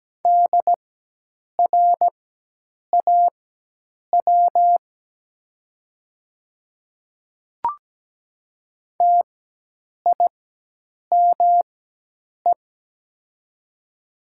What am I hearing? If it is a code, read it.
Morse code, heard as a keyed tone, about 17 wpm, spaced as for 8 wpm, 700 Hz. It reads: DRAW TIME
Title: 4x Character Spacing - Top 300 Words - 17wpm